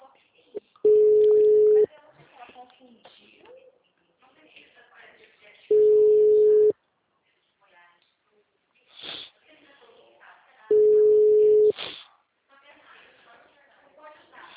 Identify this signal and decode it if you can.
Ringback tone (Europe, 425 Hz), with speech.